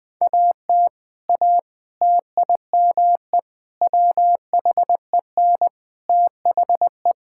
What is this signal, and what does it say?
Morse code, heard as a keyed tone, about 20 wpm, 700 Hz. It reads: AT A TIME WHEN THE
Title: Sentences with 5-Word Phrases - Spoken Beginning - 20wpm